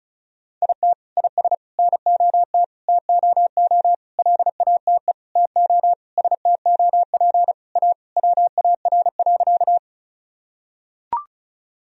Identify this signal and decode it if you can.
Morse code, heard as a keyed tone, about 34 wpm, 700 Hz. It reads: IT IS NOT TOO LATE TO STOP A WAR.
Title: Sentences from Top 1000 Words - Review Part 2 - 35wpm